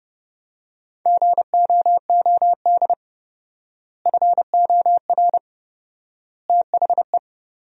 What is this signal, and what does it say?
Morse code, heard as a keyed tone, about 30 wpm, 700 Hz. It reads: GOOD FOR THE
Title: 4x Word-Spacing - Sentences from Top 300 Words - 30wpm